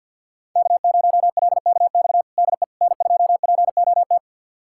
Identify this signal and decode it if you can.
Morse code, heard as a keyed tone, about 50 wpm, 700 Hz. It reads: K0LKX DE N2POT
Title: USA Callsign DE Callsign - 5 Characters - 50wpm